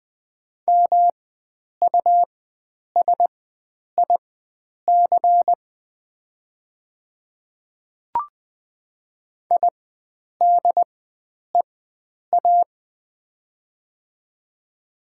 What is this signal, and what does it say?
Morse code, heard as a keyed tone, about 20 wpm, spaced as for 9 wpm, 700 Hz. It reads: MUSIC IDEA
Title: 4x Character Spacing - Top 500 Words - 20wpm